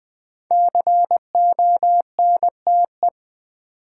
Morse code, heard as a keyed tone, about 20 wpm, 700 Hz. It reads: CONTE
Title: Build A Word - 5 Letters At A Time - 12-Letter Words - 20wpm